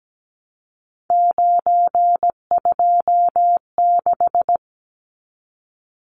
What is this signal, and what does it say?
Morse code, heard as a keyed tone, about 17 wpm, 700 Hz. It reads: 926
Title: Numbers - 3-Digit 17wpm